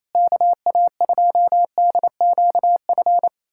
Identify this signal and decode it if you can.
Morse code, heard as a keyed tone, about 27 wpm, 700 Hz. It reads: KA2DQF
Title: USA Callsign DE Callsign - 6 Characters - 28wpm